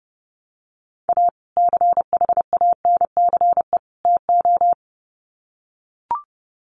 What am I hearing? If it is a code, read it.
Morse code, heard as a keyed tone, about 30 wpm, 700 Hz. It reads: A CHANCE TO
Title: Sets of 3 Words 30wpm